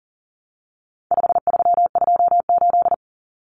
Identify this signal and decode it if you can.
Morse code, heard as a keyed tone, about 40 wpm, 700 Hz. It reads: 5328
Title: Numbers - 4-Digit 40wpm